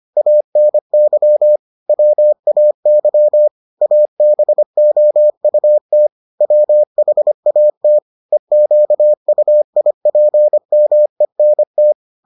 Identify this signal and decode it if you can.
Morse code, heard as a keyed tone, about 25 wpm, 600 Hz. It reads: ANY WAY ABOUT WHAT EQUIPMENT